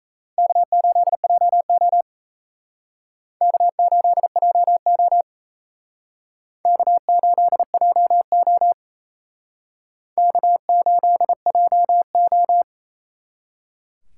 Morse code, heard as a keyed tone, about 35 wpm, 700 Hz. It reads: K8JO K8JO K8JO K8JO